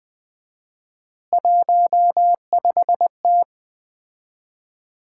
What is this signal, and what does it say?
Morse code, heard as a keyed tone, about 20 wpm, 700 Hz. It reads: 15T
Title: Common QSO Phrases - 4x Word-Spacing - 20wpm